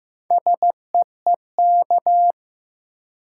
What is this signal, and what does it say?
Morse code, heard as a keyed tone, about 16 wpm, 700 Hz. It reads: SEEK
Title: Sentences from Top 1600 Words - New Word Drill - 15wpm